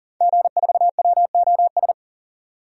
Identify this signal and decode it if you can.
Morse code, heard as a keyed tone, about 40 wpm, 700 Hz. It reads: G4WOS
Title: DXCC-100 Call Signs - 5 Characters - 40wpm